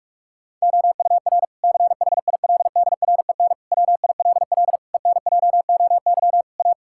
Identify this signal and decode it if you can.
Morse code, heard as a keyed tone, about 46 wpm, 700 Hz. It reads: OUR CHILDREN WILL ENJOY A
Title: Sentences from Top 1000 Words - Review Part 5 - 45wpm